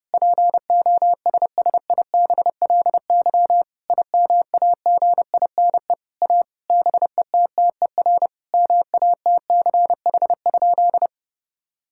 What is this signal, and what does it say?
Morse code, heard as a keyed tone, about 30 wpm, 700 Hz. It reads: POSSIBLY IMAGINE A BETTER MATCH?
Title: Sentences from Top 1600 Words 30wpm